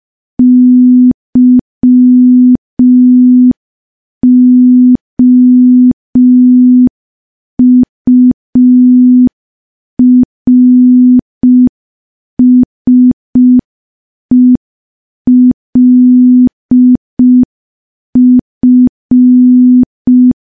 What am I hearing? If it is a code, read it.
Morse code, heard as a keyed tone, about 5 wpm, 250 Hz. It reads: YOURSELF
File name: morse / morse-training-05WPM